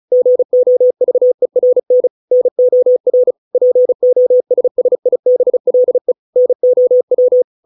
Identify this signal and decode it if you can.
Morse code, heard as a keyed tone, about 34 wpm, 500 Hz. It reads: GOVERN NOR POSSIBLE NOW